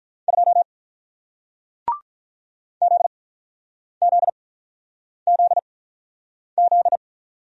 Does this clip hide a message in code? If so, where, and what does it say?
Yes: Morse code from 0.3 to 7.0 s, heard as a keyed tone, about 47 wpm, spaced as for 9 wpm, 700 Hz. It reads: 2 ZZZZ